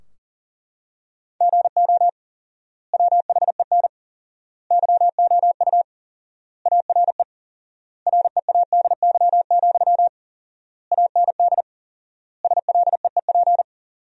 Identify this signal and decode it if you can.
Morse code, heard as a keyed tone, about 40 wpm, 700 Hz. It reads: GO WHEN YOU ARE READY, AND SLEEP
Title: 4x Word-Spacing - Sentences from Top 500 Words - 40wpm